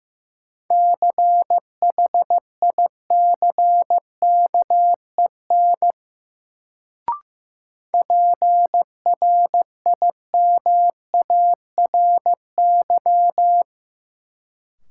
Morse code, heard as a keyed tone, about 15 wpm, 700 Hz. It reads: CHICKEN PRIMARY
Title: Sentences from Top 1000 Words - Review Part 5 - New Word Drill - 15wpm